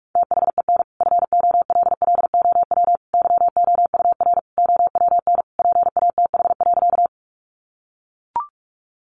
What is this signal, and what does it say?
Morse code, heard as a keyed tone, about 44 wpm, 700 Hz. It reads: THEN FOLLOW YOUR OWN PATH.
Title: Sentences from Top 1400 Words 45wpm